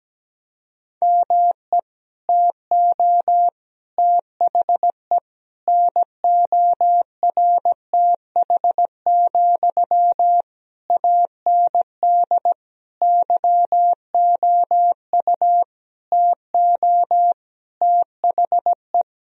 Morse code, heard as a keyed tone, about 17 wpm, 700 Hz. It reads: ME TO THE NORTH, AND YOU TO THE
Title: Sentences from Top 1000 Words - Review Part 2 - 17wpm